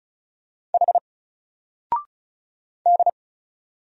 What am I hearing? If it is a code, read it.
Morse code, heard as a keyed tone, about 36 wpm, 700 Hz. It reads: H D
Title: Single Letter-Number Alt 35wpm